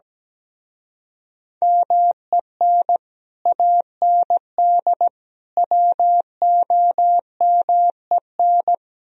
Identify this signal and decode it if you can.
Morse code, heard as a keyed tone, about 19 wpm, 700 Hz. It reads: MEN AND WOMEN
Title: Speed-Racing - Sets of 3 Words 17wpm